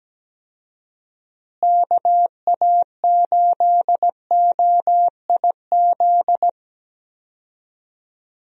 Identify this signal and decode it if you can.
Morse code, heard as a keyed tone, about 17 wpm, 700 Hz. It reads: KA8OIZ
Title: USA Call Signs - 6 Characters - Repeated 3x - 17wpm